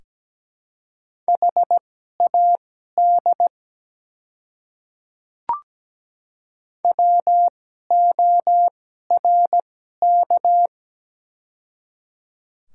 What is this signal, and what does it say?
Morse code, heard as a keyed tone, about 17 wpm, spaced as for 12 wpm, 700 Hz. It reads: HAD WORK
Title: 2x Character Spacing - Top 200 Words - 17wpm